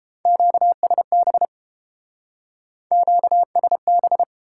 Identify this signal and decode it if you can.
Morse code, heard as a keyed tone, about 34 wpm, 700 Hz. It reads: QSB QSB